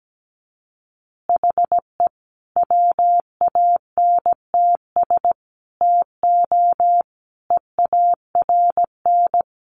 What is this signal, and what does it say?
Morse code, heard as a keyed tone, about 17 wpm, 700 Hz. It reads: HE WANTS TO EARN
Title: Additional Easy Phrases 17wpm